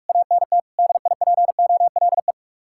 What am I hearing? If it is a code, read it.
Morse code, heard as a keyed tone, about 46 wpm, 700 Hz. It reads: ANT DIPOLE